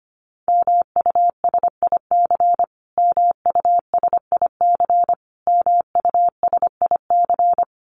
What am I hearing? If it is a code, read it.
Morse code, heard as a keyed tone, about 25 wpm, 700 Hz. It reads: MUSIC MUSIC MUSIC
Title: Top 500 Words Repeated 25wpm